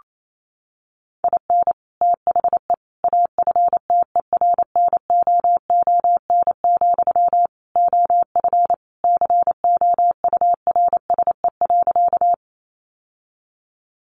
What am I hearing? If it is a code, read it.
Morse code, heard as a keyed tone, about 28 wpm, 700 Hz. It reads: IN THE AFTERNOON, OF COURSE.
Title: Sentences from Top 1500 Words 28wpm